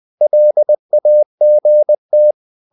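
Morse code, heard as a keyed tone, about 20 wpm, 600 Hz. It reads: LAGT
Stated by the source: Morse code practice of short Danish words at 20 wpm, from CW med Gnister.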